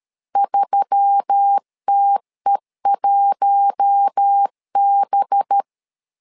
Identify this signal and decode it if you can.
Morse code, heard as a keyed tone, about 13 wpm, 800 Hz. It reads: 3TE1B
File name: LEZIONI / 5 LBVC / 5 LB 12wpm 1906